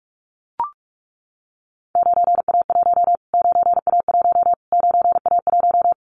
Morse code, heard as a keyed tone, about 44 wpm, 700 Hz. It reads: 9A1 9A1 9A1
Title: Callsign Prefixes for Top 25 DXCC - Repeated 3x - 45wpm